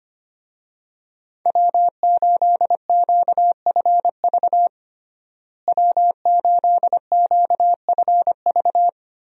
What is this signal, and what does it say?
Morse code, heard as a keyed tone, about 25 wpm, 700 Hz. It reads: W8QFV W8QFV
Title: USA Call Signs - 5 Characters - Repeated 3x - 25wpm